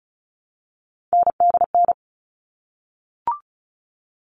Morse code, heard as a keyed tone, about 34 wpm, 700 Hz. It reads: NDN
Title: All Cut Numbers - 3-Digit 35wpm